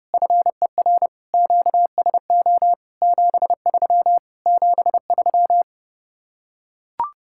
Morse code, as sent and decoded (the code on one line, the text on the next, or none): ..-. .-. / --.- ... --- / --... ...-- / --... ...--
FER QSO 73 73